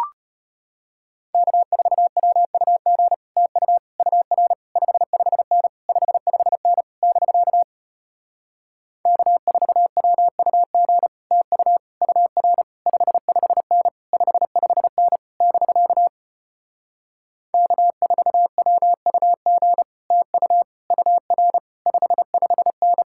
Morse code, heard as a keyed tone, about 33 wpm, 700 Hz. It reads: K4WUG TU UR 55N 55N <BK> K4WUG TU UR 55N 55N <BK> K4WUG TU UR 55N